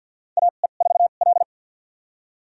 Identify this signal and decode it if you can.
Morse code, heard as a keyed tone, about 50 wpm, spaced as for 36 wpm, 700 Hz. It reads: AE4L
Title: USA Call Signs - 4 Characters - 2x Character Spacing - 50wpm